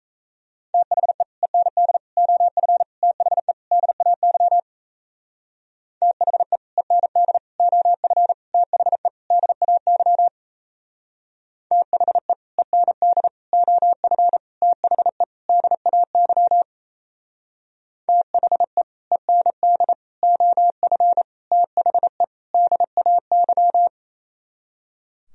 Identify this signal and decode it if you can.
Morse code, heard as a keyed tone, about 33 wpm, 700 Hz. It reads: THE END OF THE DAY THE END OF THE DAY THE END OF THE DAY THE END OF THE DAY
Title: Speed-Racing - Sets of 5 Words 28wpm